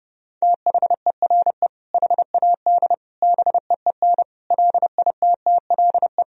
Morse code, heard as a keyed tone, about 30 wpm, 700 Hz. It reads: THERE HAD BEEN LITTLE
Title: Sentences from Top 1000 Words - Review Part 1 - 30wpm